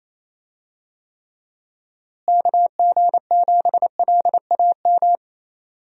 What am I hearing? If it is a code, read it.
Morse code, heard as a keyed tone, about 27 wpm, 700 Hz. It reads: KG7LAM
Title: USA Call Signs - 6 Characters - Repeated 3x - 28wpm